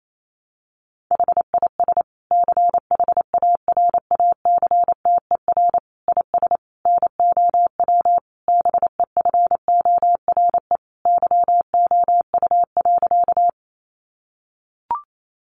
Morse code, heard as a keyed tone, about 28 wpm, 700 Hz. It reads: HIS CHARACTER IS NOW BEFORE YOU.
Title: Sentences from Top 1200 Words 28wpm